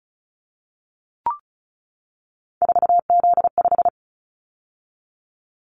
Morse code, heard as a keyed tone, about 36 wpm, 700 Hz. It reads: 4Z5
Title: Callsign Prefixes for Top 50 DXCC - 35wpm